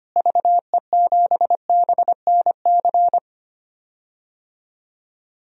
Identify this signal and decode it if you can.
Morse code, heard as a keyed tone, about 25 wpm, 700 Hz. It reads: VE7BNC
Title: DXCC-100 Call Signs - 6 Characters - Repeated 2x - 25wpm